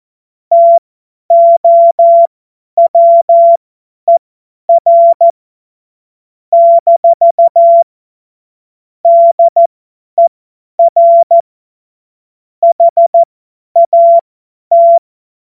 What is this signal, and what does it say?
Morse code, heard as a keyed tone, about 14 wpm, spaced as for 10 wpm, 680 Hz. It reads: TOWER - DER HAT
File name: Morsekurs / Witze / Witz 25. April 2023 50 F70